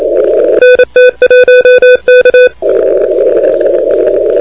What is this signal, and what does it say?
Morse code, heard as a keyed tone, about 27 wpm, 2500 Hz, with noise. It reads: NT1K
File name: recordings / PA3EVY de NT1K ARRL10M CW 28MHz